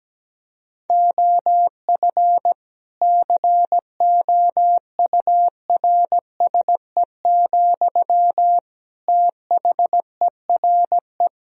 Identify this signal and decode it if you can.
Morse code, heard as a keyed tone, about 17 wpm, 700 Hz. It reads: OF COURSE, THERE
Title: Sentences from Top 1000 Words - Review Part 4 - 17wpm